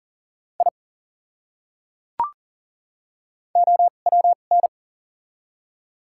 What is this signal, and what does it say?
Morse code, heard as a keyed tone, about 38 wpm, spaced as for 28 wpm, 700 Hz. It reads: I OWN
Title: 2x Character Spacing - Top 500 Words - 40wpm